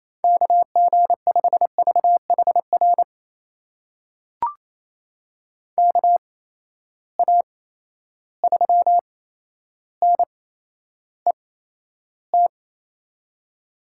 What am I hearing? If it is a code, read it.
Morse code, heard as a keyed tone, about 28 wpm, spaced as for 12 wpm, 700 Hz. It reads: KG5VHR KA3NET
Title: USA Call Signs - 6 Characters - 8x Character Spacing - 28wpm